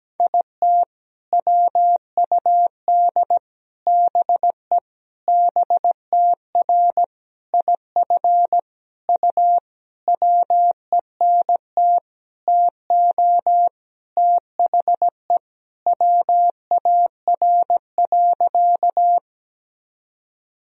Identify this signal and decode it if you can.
Morse code, heard as a keyed tone, about 17 wpm, 700 Hz. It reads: IT WUD BE BTR IF U WENT TO THE WAR.
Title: Sentences from Top 400 Words and Abbreviations 17wpm